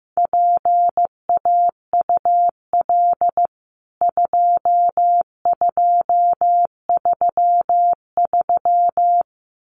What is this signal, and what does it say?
Morse code, heard as a keyed tone, about 15 wpm, 700 Hz. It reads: PAUL 2233
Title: CWT Exchange 15wpm